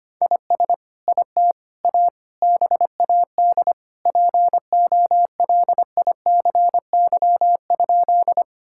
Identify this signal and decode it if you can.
Morse code, heard as a keyed tone, about 25 wpm, 700 Hz. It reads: IS IT A BAD POLICY?
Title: Sentences from Top 1000 Words - Review Part 5 - 25wpm